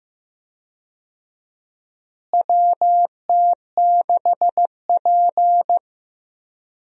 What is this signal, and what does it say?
Morse code, heard as a keyed tone, about 15 wpm, 700 Hz. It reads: WT6P
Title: USA Call Signs - 4 Characters - Repeated 2x - 15wpm